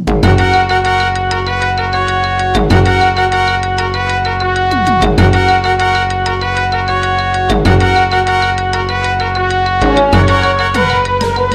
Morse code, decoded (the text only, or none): none